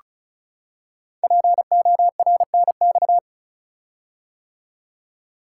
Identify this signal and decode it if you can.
Morse code, heard as a keyed tone, about 36 wpm, 690 Hz. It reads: PORNX
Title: Code Groups - 5 Letters - 35wpm